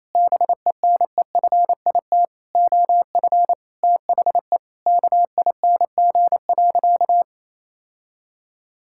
Morse code, heard as a keyed tone, about 28 wpm, 700 Hz. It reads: BENEFIT OF THE KING.
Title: Sentences from Top 1000 Words - Review Part 6 - 28wpm